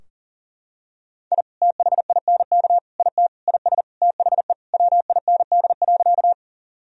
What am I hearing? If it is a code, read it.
Morse code, heard as a keyed tone, about 40 wpm, 700 Hz. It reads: I THINK IT IS THE WIND.
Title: Sentences from Top 1000 Words - Review Part 2 - 40wpm